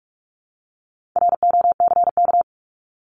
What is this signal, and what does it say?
Morse code, heard as a keyed tone, about 44 wpm, 700 Hz. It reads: ROCK
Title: Top 500 Words Repeated 45wpm